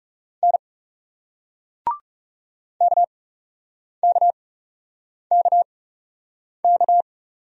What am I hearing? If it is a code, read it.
Morse code, heard as a keyed tone, about 36 wpm, spaced as for 8 wpm, 700 Hz. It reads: N KKKK
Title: Speed-Racing Single Letters 30wpm